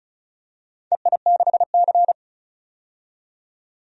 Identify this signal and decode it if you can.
Morse code, heard as a keyed tone, about 34 wpm, 700 Hz. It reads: EI6C
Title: DXCC-100 Call Signs - 4 Characters - 35wpm